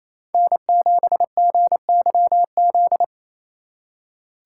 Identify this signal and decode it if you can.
Morse code, heard as a keyed tone, about 36 wpm, 700 Hz. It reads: N7GYZ